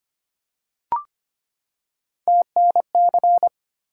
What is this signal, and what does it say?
Morse code, heard as a keyed tone, about 25 wpm, 700 Hz. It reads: TNC
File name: Three Letter-Number Combinations - 25wpm